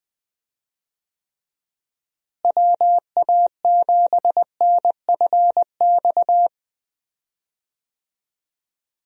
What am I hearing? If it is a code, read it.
Morse code, heard as a keyed tone, about 21 wpm, 700 Hz. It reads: WA7NFX
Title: USA Call Signs - 6 Characters - Repeated 2x - 20wpm